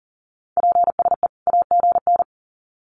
Morse code, heard as a keyed tone, about 40 wpm, 700 Hz. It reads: PSE AGN
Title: Common QSO Phrases 40wpm